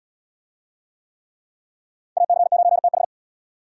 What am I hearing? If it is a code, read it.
Morse code, heard as a keyed tone, about 70 wpm, 700 Hz. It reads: AC9TH